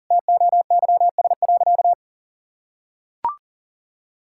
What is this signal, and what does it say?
Morse code, heard as a keyed tone, about 40 wpm, 700 Hz. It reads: TOYS.
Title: Pangram Challenge v2 - 40wpm